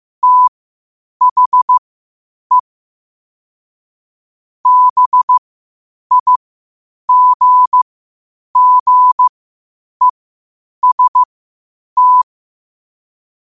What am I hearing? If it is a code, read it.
Morse code, heard as a keyed tone, about 15 wpm, spaced as for 8 wpm, 1000 Hz. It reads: THE BIGGEST
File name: qotdfiles / 8wpm